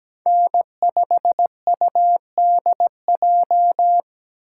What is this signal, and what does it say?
Morse code, heard as a keyed tone, about 17 wpm, 700 Hz. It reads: N5UDJ
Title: USA Call Signs - 5 Characters - 3x Character Spacing - 17wpm